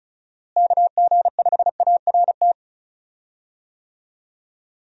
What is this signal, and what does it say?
Morse code, heard as a keyed tone, about 36 wpm, 700 Hz. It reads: KG5ART
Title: USA Call Signs - 6 Characters - Repeated 2x - 35wpm